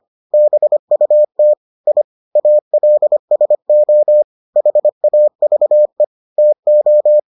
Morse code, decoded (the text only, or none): BUT I ALSO HAVE TO